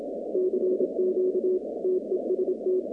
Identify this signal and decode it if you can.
Morse code, heard as a keyed tone, about 26 wpm, 370 Hz, with noise. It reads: CQ TEST
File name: contests / 2019-cw-wpx-kt5j-k5pi- -2330 0-47